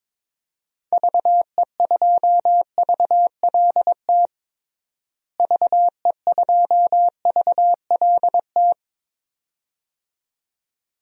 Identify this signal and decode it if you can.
Morse code, heard as a keyed tone, about 22 wpm, 700 Hz. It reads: VE2VLT VE2VLT
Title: DXCC-100 Call Signs - 6 Characters - Repeated 2x - 22wpm